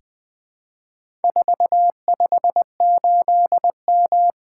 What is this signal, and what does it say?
Morse code, heard as a keyed tone, about 20 wpm, 700 Hz. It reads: 458M
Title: Four Letter-Number Combinations - 20wpm